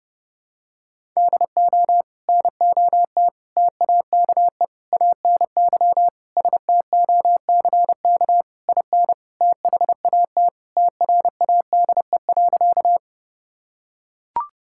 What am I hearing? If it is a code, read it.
Morse code, heard as a keyed tone, about 32 wpm, 700 Hz. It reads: DO NOT TAKE ANY STOCK IN THAT TRADE.
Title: Sentences from Top 1000 Words - Review Part 5 - 30wpm